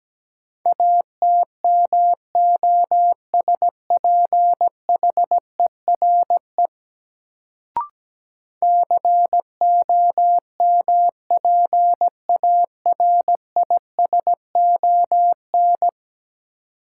Morse code, heard as a keyed tone, about 17 wpm, 700 Hz. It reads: ATMOSPHERE COMPARISON
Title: Sentences from Top 1600 Words - New Word Drill - 17wpm